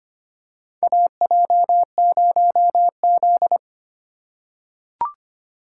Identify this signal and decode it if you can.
Morse code, heard as a keyed tone, about 25 wpm, 700 Hz. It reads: AJ0Z
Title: USA Call Signs - 4 Characters - 4x Character Spacing - 25wpm